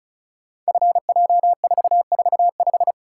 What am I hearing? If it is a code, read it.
Morse code, heard as a keyed tone, about 36 wpm, 690 Hz. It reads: FJ445
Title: Code Groups - 5 Letter:Numbers - 35wpm